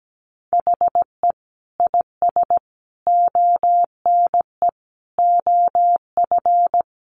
Morse code, heard as a keyed tone, about 17 wpm, 700 Hz. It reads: HE IS ONE OF
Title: Sets of 5 Words Encore 17wpm